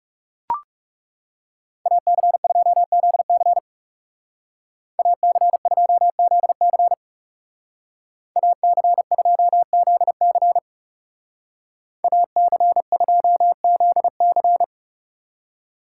Morse code, heard as a keyed tone, about 38 wpm, 700 Hz. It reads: AC2ZC AC2ZC AC2ZC AC2ZC
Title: Speed Racing - USA Call Signs 30wpm